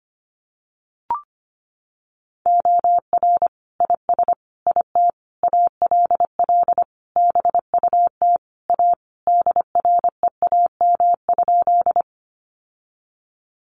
Morse code, heard as a keyed tone, about 25 wpm, 700 Hz. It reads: OR IS IT ALL BUT A DREAM?
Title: Sentences from Top 1500 Words 25wpm